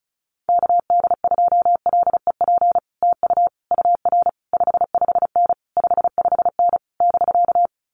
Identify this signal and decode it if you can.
Morse code, heard as a keyed tone, about 34 wpm, 700 Hz. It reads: KD2LEP TU UR 55N 55N <BK>
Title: Common QSO Phrases Encore 35wpm